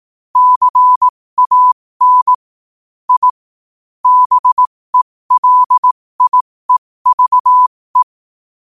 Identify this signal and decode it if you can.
Morse code, heard as a keyed tone, about 18 wpm, spaced as for 15 wpm, 1000 Hz. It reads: CAN I BELIEVE
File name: qotdfiles / 15wpm